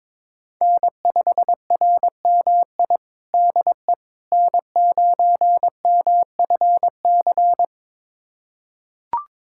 Morse code, heard as a keyed tone, about 21 wpm, 700 Hz. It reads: N5RMI DE N9MFC
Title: USA Callsign DE Callsign - 5 Characters - 22wpm